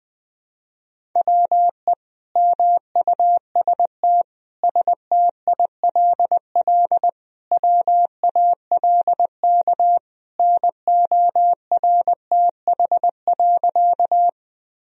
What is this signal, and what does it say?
Morse code, heard as a keyed tone, about 21 wpm, 700 Hz. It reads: WE MUST STILL WALK NORTH.
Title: Sentences from Top 1000 Words - Review Part 1 - 20wpm